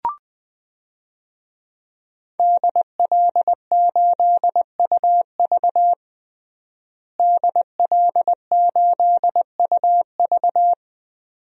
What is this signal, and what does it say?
Morse code, heard as a keyed tone, about 20 wpm, 700 Hz. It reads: DL8UV DL8UV
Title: DXCC-100 Call Signs - 5 Characters - Repeated 2x - 20wpm